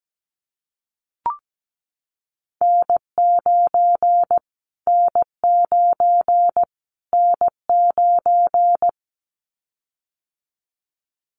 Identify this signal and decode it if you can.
Morse code, heard as a keyed tone, about 17 wpm, 700 Hz. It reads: N9 N9 N9
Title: Callsign Prefixes for Top 25 DXCC - Repeated 3x - 17wpm